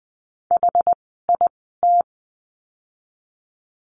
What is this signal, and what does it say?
Morse code, heard as a keyed tone, about 20 wpm, spaced as for 15 wpm, 700 Hz. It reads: HIT
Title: 061 Lesson 11f - Letter h - Words 20wpm